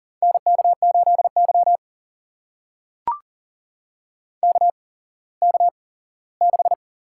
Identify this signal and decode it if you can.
Morse code, heard as a keyed tone, about 40 wpm, spaced as for 19 wpm, 700 Hz. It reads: NK8Y KK6
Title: USA Call Signs - 4 Characters - 8x Character Spacing - 40wpm